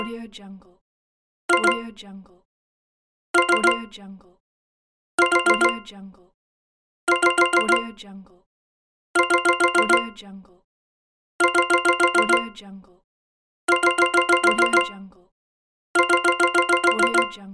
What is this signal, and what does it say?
Speech. There is no code.